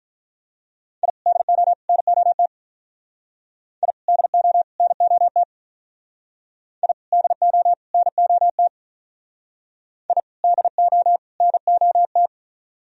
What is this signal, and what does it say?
Morse code, heard as a keyed tone, about 46 wpm, 700 Hz. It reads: I DO NOT I DO NOT I DO NOT I DO NOT